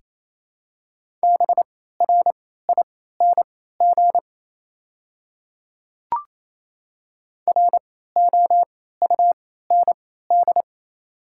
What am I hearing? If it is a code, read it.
Morse code, heard as a keyed tone, about 28 wpm, spaced as for 16 wpm, 700 Hz. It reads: BRING ROUND
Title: 3x Character Spacing - Top 500 Words - 28wpm